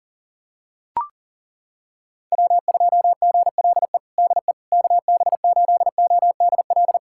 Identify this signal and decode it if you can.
Morse code, heard as a keyed tone, about 40 wpm, 700 Hz. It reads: W2GLE DE KB8ODL
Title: USA Callsign DE Callsign 40wpm